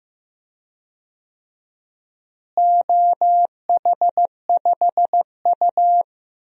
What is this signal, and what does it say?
Morse code, heard as a keyed tone, about 15 wpm, 700 Hz. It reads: OH5U
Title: DXCC-100 Call Signs - 4 Characters - Repeated 3x - 15wpm